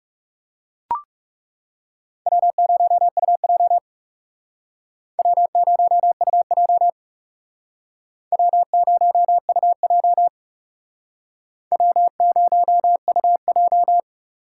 Morse code, heard as a keyed tone, about 39 wpm, 700 Hz. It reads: W0UJ W0UJ W0UJ W0UJ